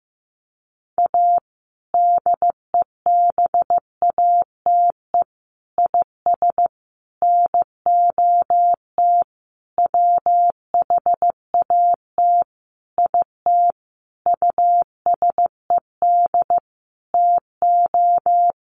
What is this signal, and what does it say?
Morse code, heard as a keyed tone, about 15 wpm, 700 Hz. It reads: A DEBATE IS NOT WHAT IT USED TO
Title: Sentences from Top 1100 Words 15wpm